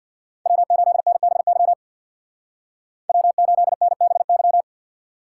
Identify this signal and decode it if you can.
Morse code, heard as a keyed tone, about 58 wpm, 700 Hz. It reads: W8NBY W8NBY
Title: Speed Racing - USA 1x3 Call Signs 50wpm